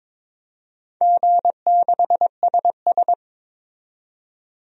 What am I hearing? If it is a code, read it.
Morse code, heard as a keyed tone, about 22 wpm, 700 Hz. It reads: G6SS
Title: DXCC-100 Call Signs - 4 Characters - 22wpm